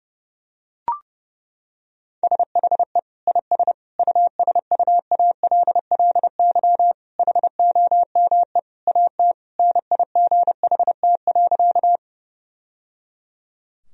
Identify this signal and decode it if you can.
Morse code, heard as a keyed tone, about 30 wpm, 700 Hz. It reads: SHE IS USUALLY HOME AT NIGHT.
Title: Sentences from Top 1000 Words - Review Part 3 - 30wpm